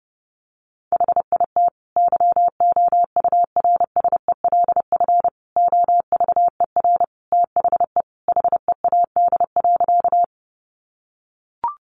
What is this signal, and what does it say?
Morse code, heard as a keyed tone, about 30 wpm, 700 Hz. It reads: HIT YOURSELF OVER THE HEAD.
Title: Sentences from Top 1200 Words 30wpm